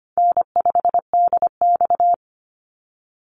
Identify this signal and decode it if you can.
Morse code, heard as a keyed tone, about 25 wpm, 700 Hz. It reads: N5DX
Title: Easy Call Signs Repeated 2x - 25wpm